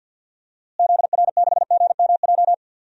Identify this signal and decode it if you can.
Morse code, heard as a keyed tone, about 51 wpm, 700 Hz. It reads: ZR6GMJ
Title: DXCC-100 Call Signs - 6 Characters - Repeated 3x - 50wpm